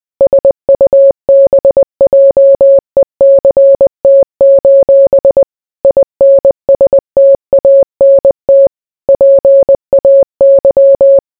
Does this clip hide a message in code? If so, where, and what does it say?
Yes: Morse code from 0.2 to 11.2 s, heard as a keyed tone, about 20 wpm, 550 Hz. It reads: SUBJECT: INSTANT PAY